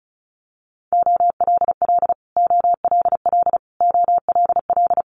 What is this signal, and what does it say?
Morse code, heard as a keyed tone, about 34 wpm, 700 Hz. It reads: OLL OLL OLL
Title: Multi-letter Phonemes Repeated 3x 35wpm